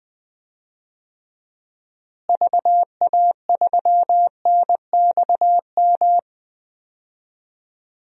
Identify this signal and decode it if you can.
Morse code, heard as a keyed tone, about 20 wpm, 700 Hz. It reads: VA3NXM
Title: DXCC-100 Call Signs - 6 Characters - Repeated 2x - 20wpm